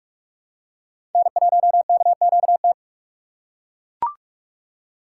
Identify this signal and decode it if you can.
Morse code, heard as a keyed tone, about 42 wpm, 700 Hz. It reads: N1KQT
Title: USA Call Signs - 5 Characters - 2x Character Spacing - 45wpm